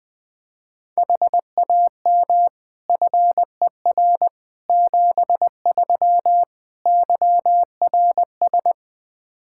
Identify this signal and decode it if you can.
Morse code, heard as a keyed tone, about 21 wpm, 700 Hz. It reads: HAM FER 73 YRS
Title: Practice QSO - Exchange 3 - More Info - 20wpm